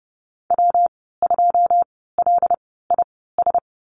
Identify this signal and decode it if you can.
Morse code, heard as a keyed tone, about 30 wpm, spaced as for 17 wpm, 700 Hz. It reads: W2LIS
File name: Lesson 19g - Letter w - Call Signs 30wpm